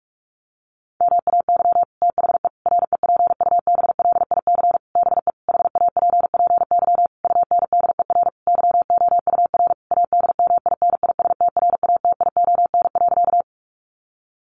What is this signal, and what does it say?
Morse code, heard as a keyed tone, about 44 wpm, 700 Hz. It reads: MAY THE REPUBLIC BE HAPPY UNDER YOUR ADMINISTRATION.
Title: Sentences from Top 1500 Words 45wpm